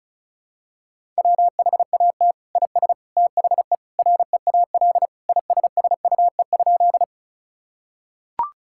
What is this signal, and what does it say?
Morse code, heard as a keyed tone, about 36 wpm, 700 Hz. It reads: WHAT IS THE REAL ISSUE?
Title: Sentences from Top 1000 Words - Review Part 6 - 35wpm